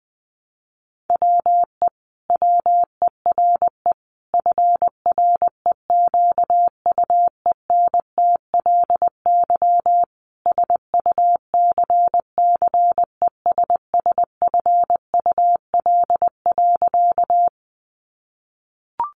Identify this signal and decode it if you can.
Morse code, heard as a keyed tone, about 20 wpm, 700 Hz. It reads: WE WERE FREQUENTLY SUCCESSFUL.
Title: Sentences from Top 1300 Words 20wpm